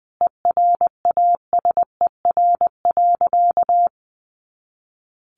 Morse code, heard as a keyed tone, about 20 wpm, 700 Hz. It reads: ERASER.
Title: Additional Easy Phrases 20wpm